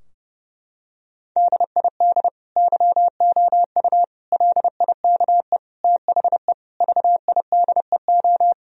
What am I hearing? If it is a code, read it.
Morse code, heard as a keyed tone, about 30 wpm, 700 Hz. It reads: DID YOU LIKE THE VIDEO
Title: Sentences from Top 1000 Words - Review Part 4 - 30wpm